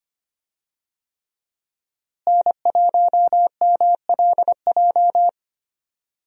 Morse code, heard as a keyed tone, about 25 wpm, 700 Hz. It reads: N1MLJ
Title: USA Call Signs - 5 Characters - Repeated 3x - 25wpm